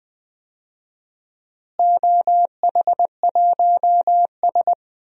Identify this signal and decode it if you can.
Morse code, heard as a keyed tone, about 20 wpm, 700 Hz. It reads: OH1S
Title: DXCC-100 Call Signs - 4 Characters - Repeated 3x - 20wpm